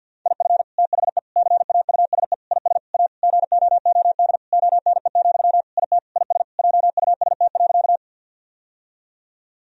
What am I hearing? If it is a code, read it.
Morse code, heard as a keyed tone, about 51 wpm, 700 Hz. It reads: IF THE CAUSE IS A GOOD ONE, IT IS JUST.
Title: Sentences from Top 1000 Words - Review Part 1 - 50wpm